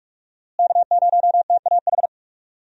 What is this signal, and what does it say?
Morse code, heard as a keyed tone, about 47 wpm, 700 Hz. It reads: K0TAH
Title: USA Call Signs - 5 Characters - Repeated 3x - 45wpm